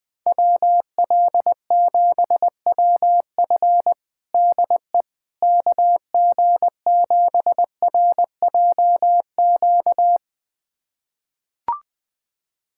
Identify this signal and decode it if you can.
Morse code, heard as a keyed tone, about 21 wpm, 700 Hz. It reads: WL7WF DE KG7RJQ
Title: USA Callsign DE Callsign 20wpm